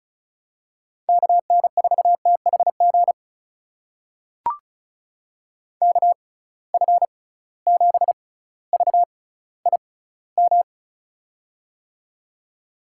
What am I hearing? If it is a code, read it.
Morse code, heard as a keyed tone, about 36 wpm, spaced as for 18 wpm, 700 Hz. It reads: KN4THG KF7VIM
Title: USA Call Signs - 6x Character Spacing - 35wpm